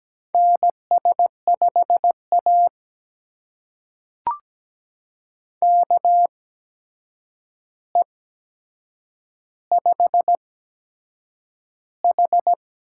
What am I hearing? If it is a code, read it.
Morse code, heard as a keyed tone, about 17 wpm, spaced as for 7 wpm, 700 Hz. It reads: NS5A KE5H
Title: USA Call Signs - 8x Character Spacing - 17wpm